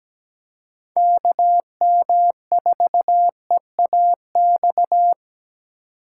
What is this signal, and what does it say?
Morse code, heard as a keyed tone, about 17 wpm, 700 Hz. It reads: KM4EAX
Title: USA Call Signs - 6 Characters - 4x Character Spacing - 17wpm